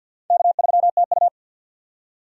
Morse code, heard as a keyed tone, about 48 wpm, 700 Hz. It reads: K3TU
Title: USA Call Signs - 4 Characters - 50wpm